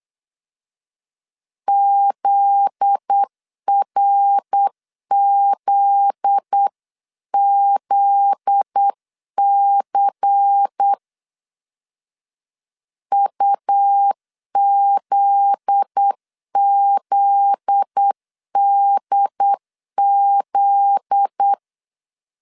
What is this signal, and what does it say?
Morse code, heard as a keyed tone, about 8 wpm, 800 Hz. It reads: ZRZZC UZZDZ
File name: LEZIONI / 7 GF9Z / 7 GF9Z 8wpm 3213